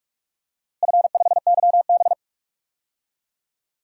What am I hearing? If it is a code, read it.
Morse code, heard as a keyed tone, about 41 wpm, 700 Hz. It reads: F5YB
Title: DXCC-100 Call Signs - 4 Characters - Repeated 3x - 45wpm